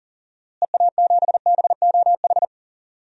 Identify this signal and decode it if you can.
Morse code, heard as a keyed tone, about 40 wpm, 700 Hz. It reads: EA7BOH
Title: DXCC-100 Call Signs - 6 Characters - Repeated 3x - 40wpm